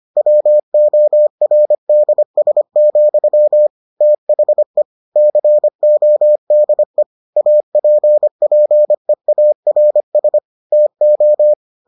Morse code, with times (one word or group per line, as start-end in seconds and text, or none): 0.2-3.7 WORDS,
4.0-4.8 THE
5.2-7.0 CODE
7.4-10.4 APPEARS
10.7-11.5 TO